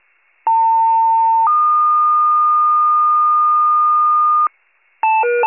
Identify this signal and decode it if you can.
Two-tone page, heard as a one-second tone, then a three-second tone, about 900 then 1220 Hz. It is heard from 0.5 to 4.5 s.